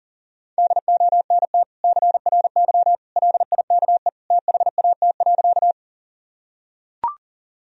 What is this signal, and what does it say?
Morse code, heard as a keyed tone, about 40 wpm, 700 Hz. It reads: DONT CRY LIKE THAT.
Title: Sentences from Top 1000 Words - Review Part 2 - 40wpm